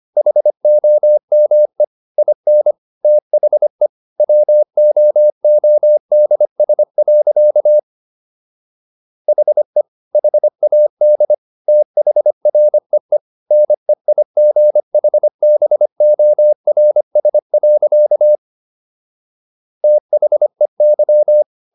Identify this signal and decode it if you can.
Morse code, heard as a keyed tone, about 25 wpm, 600 Hz. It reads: HOME IN THE WOODS. HE HAD THREE NEIGHBORS. THEY